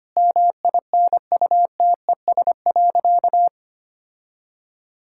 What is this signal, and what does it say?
Morse code, heard as a keyed tone, about 25 wpm, 700 Hz. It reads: MINUTES.
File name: practice / 25wpm / Sentences with 3-Word Phrases - 25wpm